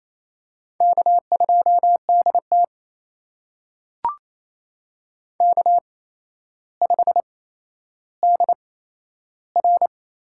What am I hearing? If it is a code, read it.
Morse code, heard as a keyed tone, about 28 wpm, spaced as for 12 wpm, 700 Hz. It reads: K2DT K5DR
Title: USA Call Signs - 4 Characters - 8x Character Spacing - 28wpm